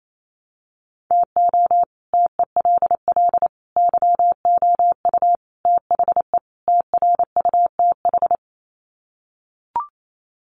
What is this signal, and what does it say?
Morse code, heard as a keyed tone, about 28 wpm, 700 Hz. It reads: TO TELL YOU THE TRUTH
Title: Sets of 5 Words 28wpm